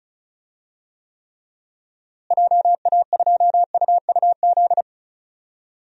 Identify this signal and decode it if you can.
Morse code, heard as a keyed tone, about 36 wpm, 700 Hz. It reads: JA2UUZ
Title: DXCC-100 Call Signs - 6 Characters - Repeated 3x - 35wpm